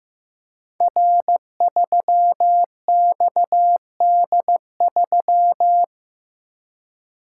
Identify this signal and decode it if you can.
Morse code, heard as a keyed tone, about 15 wpm, 700 Hz. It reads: R3XD3
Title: Code Groups - 5 Letter:Numbers - 15wpm